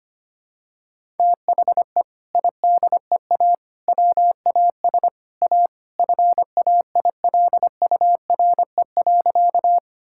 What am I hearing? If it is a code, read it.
Morse code, heard as a keyed tone, about 25 wpm, 700 Hz. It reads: THE IDEA WAS A FAILURE.
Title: Sentences from Top 1000 Words - Review Part 6 - 25wpm